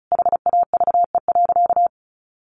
Morse code, heard as a keyed tone, about 34 wpm, 700 Hz. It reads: HAVE.
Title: Sentences from Top 1200 Words 35wpm